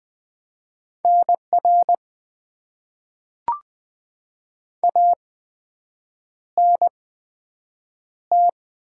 Morse code, heard as a keyed tone, about 20 wpm, spaced as for 7 wpm, 700 Hz. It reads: NR ANT
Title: Top 90 QSO Elements - 8x Character Spacing - 20wpm